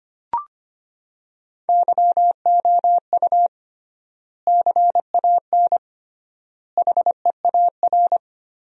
Morse code, heard as a keyed tone, about 25 wpm, 700 Hz. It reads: YOU CAN HEAR
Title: 3x Word-Spacing - Sentences from Top 500 Words - 25wpm